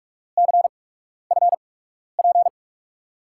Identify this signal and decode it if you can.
Morse code, heard as a keyed tone, about 48 wpm, spaced as for 12 wpm, 700 Hz. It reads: CFP
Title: USA Call Signs - 6 Characters - 8x Character Spacing - 45wpm